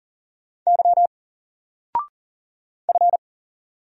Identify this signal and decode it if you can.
Morse code, heard as a keyed tone, about 39 wpm, 700 Hz. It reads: Y F
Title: Single Letter-Number 40wpm